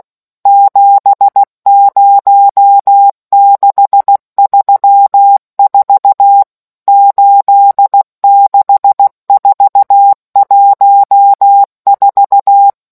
Morse code, heard as a keyed tone, about 16 wpm, 800 Hz. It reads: 70634 86414